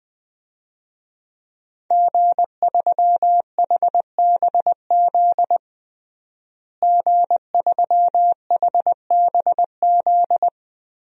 Morse code, heard as a keyed tone, about 21 wpm, 700 Hz. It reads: G3HBZ G3HBZ
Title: DXCC-100 Call Signs - 5 Characters - Repeated 2x - 20wpm